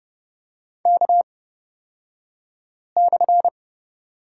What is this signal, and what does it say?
Morse code, heard as a keyed tone, about 30 wpm, spaced as for 5 wpm, 700 Hz. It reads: K/
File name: Lesson 37d - Letter x - Two Characters 30wpm